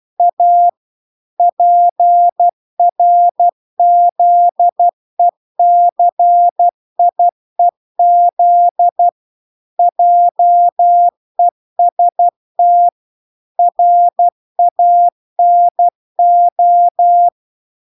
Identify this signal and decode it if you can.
Morse code, heard as a keyed tone, about 12 wpm, 700 Hz. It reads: A PRZECIEZ JEST RANO